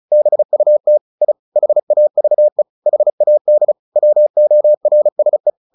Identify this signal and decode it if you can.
Morse code, heard as a keyed tone, about 35 wpm, 600 Hz. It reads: BUT I HAVE HAD WORSE
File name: the old man and the sea 2 35wpm